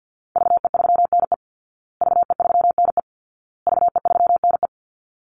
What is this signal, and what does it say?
Morse code, heard as a keyed tone, about 50 wpm, 700 Hz. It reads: VE3NE VE3NE VE3NE
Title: Call Signs Repeated 2x - 50wpm